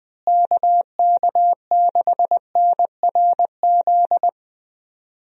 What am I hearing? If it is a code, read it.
Morse code, heard as a keyed tone, about 20 wpm, 700 Hz. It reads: KK6NRZ
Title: USA Call Signs - 6 Characters - Repeated 2x - 20wpm